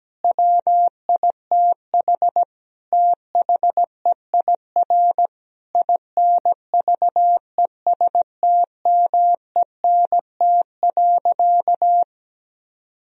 Morse code, heard as a keyed tone, about 17 wpm, 700 Hz. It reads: WITH THEIR INVESTMENT.
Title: Sentences from Top 1000 Words - Review Part 4 - 17wpm